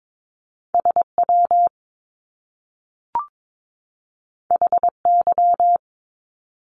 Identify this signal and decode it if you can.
Morse code, heard as a keyed tone, about 22 wpm, 700 Hz. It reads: SW HY
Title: Top Two Letter Combinations in Words 22wpm